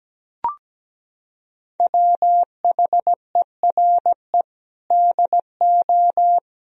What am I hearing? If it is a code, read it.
Morse code, heard as a keyed tone, about 17 wpm, 700 Hz. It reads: WHERE DO
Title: Sentences from Top 200 Words and Abbreviations 17wpm